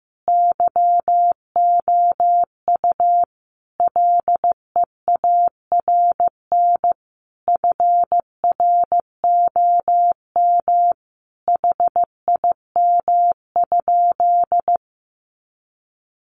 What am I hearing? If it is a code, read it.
Morse code, heard as a keyed tone, about 15 wpm, 700 Hz. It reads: YOU LEARN FROM HIM?
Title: Sentences from Top 300 Words 15wpm